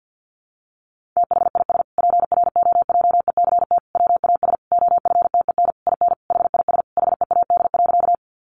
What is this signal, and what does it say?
Morse code, heard as a keyed tone, about 50 wpm, 700 Hz. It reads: THIS PROJECT WAS OFTEN IN HIS HEAD.
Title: Sentences from Top 1400 Words 50wpm